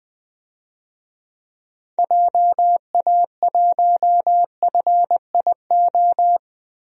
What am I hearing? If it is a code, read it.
Morse code, heard as a keyed tone, about 20 wpm, 700 Hz. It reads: JA1FIO
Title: DXCC-100 Call Signs - 6 Characters - 20wpm